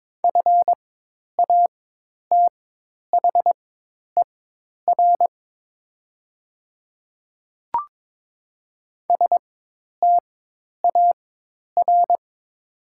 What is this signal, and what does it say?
Morse code, heard as a keyed tone, about 22 wpm, spaced as for 10 wpm, 700 Hz. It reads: FATHER STAR
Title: 4x Character Spacing - Top 500 Words - 22wpm